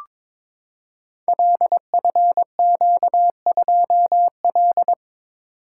Morse code, heard as a keyed tone, about 22 wpm, 700 Hz. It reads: LFQ2L
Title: Code Groups - 5 Letter:Numbers - 22wpm